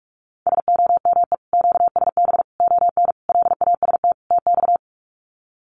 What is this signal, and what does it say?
Morse code, heard as a keyed tone, about 44 wpm, 700 Hz. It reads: SOME QSB ON LAST TX
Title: Common QSO Phrases 45wpm